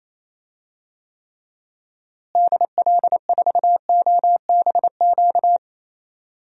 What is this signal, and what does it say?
Morse code, heard as a keyed tone, about 28 wpm, 700 Hz. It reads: DL4OBQ
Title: USA Call Signs - 6 Characters - Repeated 3x - 28wpm